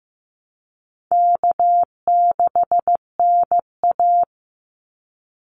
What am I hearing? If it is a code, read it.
Morse code, heard as a keyed tone, about 15 wpm, 700 Hz. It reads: K6NA
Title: Easy Call Signs Repeated 2x - 15wpm